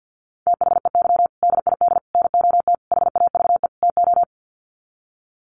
Morse code, heard as a keyed tone, about 50 wpm, 700 Hz. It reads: THEY DID NOT HAVE TO
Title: Sets of 5 Words Encore2 50wpm